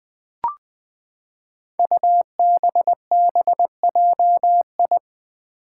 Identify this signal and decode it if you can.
Morse code, heard as a keyed tone, about 20 wpm, 700 Hz. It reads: UBBJI
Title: Five Challenging Letter-Number Combinations 20wpm